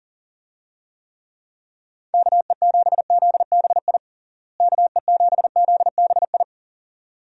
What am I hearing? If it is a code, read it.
Morse code, heard as a keyed tone, about 40 wpm, 700 Hz. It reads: KE7ZBI KE7ZBI
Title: USA Call Signs - 6 Characters - Repeated 2x - 40wpm